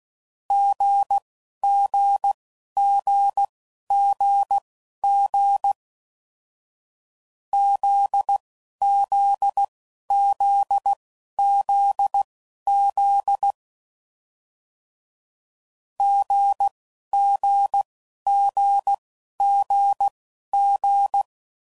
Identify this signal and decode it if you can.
Morse code, heard as a keyed tone, about 16 wpm, spaced as for 11 wpm, 800 Hz. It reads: GGGGG ZZZZZ GGGGG
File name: Mixed / Short Runs / GZ GZ7